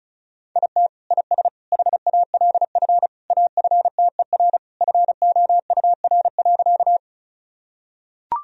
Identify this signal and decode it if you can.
Morse code, heard as a keyed tone, about 34 wpm, 700 Hz. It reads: IT IS HALF AFTER FOUR.
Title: Sentences from Top 1000 Words - Review Part 1 - 35wpm